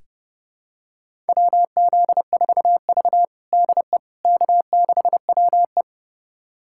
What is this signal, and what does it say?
Morse code, heard as a keyed tone, about 32 wpm, 700 Hz. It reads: WZ4V DE K6WE
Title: USA Callsign DE Callsign - 4 Characters - 30wpm